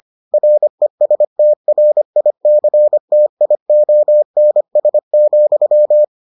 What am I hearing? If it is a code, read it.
Morse code, heard as a keyed tone, about 25 wpm, 600 Hz. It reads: RESTRICTIONS,